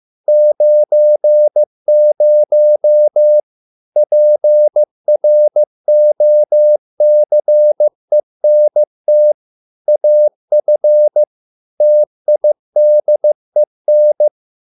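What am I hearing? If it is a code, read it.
Morse code, heard as a keyed tone, about 15 wpm, 600 Hz. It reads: 90 PROCENT AF TIDEN
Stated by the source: Morse code at a slow 15 wpm from CW med Gnister.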